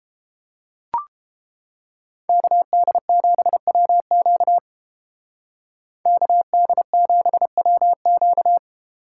Morse code, heard as a keyed tone, about 30 wpm, 700 Hz. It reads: KD7WQ KD7WQ